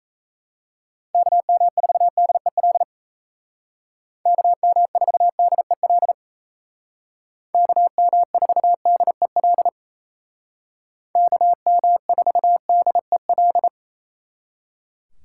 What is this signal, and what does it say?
Morse code, heard as a keyed tone, about 37 wpm, 700 Hz. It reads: KM4DEL KM4DEL KM4DEL KM4DEL